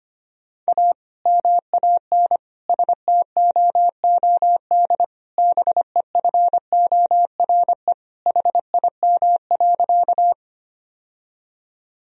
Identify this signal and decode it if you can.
Morse code, heard as a keyed tone, about 25 wpm, 700 Hz. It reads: A MAN STOOD BEFORE HIM.
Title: Sentences from Top 1000 Words - Review Part 2 - 25wpm